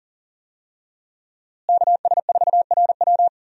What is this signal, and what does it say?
Morse code, heard as a keyed tone, about 40 wpm, 700 Hz. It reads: KS4RW
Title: USA Call Signs - 5 Characters - Repeated 3x - 40wpm